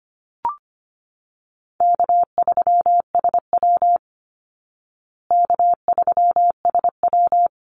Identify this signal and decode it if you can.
Morse code, heard as a keyed tone, about 25 wpm, 700 Hz. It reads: K3SW K3SW
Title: Call Signs Repeated 3x - 25wpm